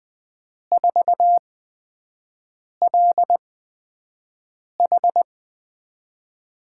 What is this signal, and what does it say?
Morse code, heard as a keyed tone, about 20 wpm, spaced as for 5 wpm, 700 Hz. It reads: 4LH
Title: USA Call Signs - 8x Character Spacing - 20wpm